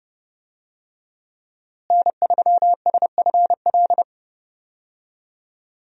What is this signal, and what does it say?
Morse code, heard as a keyed tone, about 28 wpm, 700 Hz. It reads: N3SFL
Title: USA Call Signs - 5 Characters - Repeated 3x - 30wpm